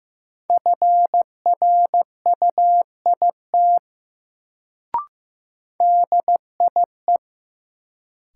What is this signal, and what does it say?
Morse code, heard as a keyed tone, about 15 wpm, 700 Hz. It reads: FRUIT DIE
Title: Sentences from Top 1300 Words - New Word Drill Alt - 15wpm